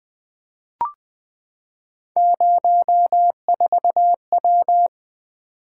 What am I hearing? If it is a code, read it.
Morse code, heard as a keyed tone, about 20 wpm, 700 Hz. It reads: 04W
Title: Three Letter-Number Combinations - 20wpm